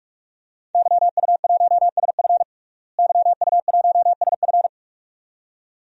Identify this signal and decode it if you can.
Morse code, heard as a keyed tone, about 46 wpm, 700 Hz. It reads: YU1SF YU1SF
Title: DXCC-100 Call Signs - 5 Characters - Repeated 2x - 45wpm